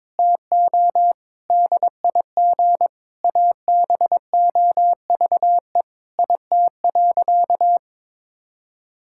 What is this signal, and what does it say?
Morse code, heard as a keyed tone, about 22 wpm, 700 Hz. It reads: TO DIG ABOVE IT.
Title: Sentences from Top 1600 Words 22wpm